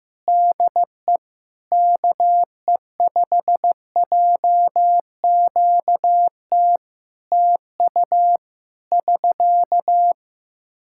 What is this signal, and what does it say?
Morse code, heard as a keyed tone, about 15 wpm, 700 Hz. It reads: DE KE5JQT TU <SK>
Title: Practice QSO - Exchange 4 - Ending - 15wpm